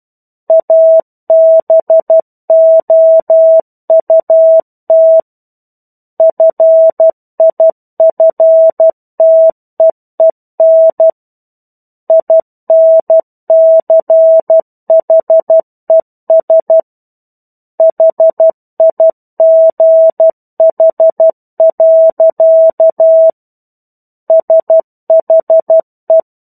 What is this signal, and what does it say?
Morse code, heard as a keyed tone, about 12 wpm, 650 Hz. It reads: ABOUT FIFTEEN INCHES HIGH. SHE
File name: Chapter-01 12 WPM